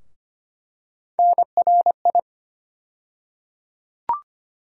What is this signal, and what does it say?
Morse code, heard as a keyed tone, about 25 wpm, 700 Hz. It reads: NRI
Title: USA Call Sign Postfixes - 3 Letters 25wpm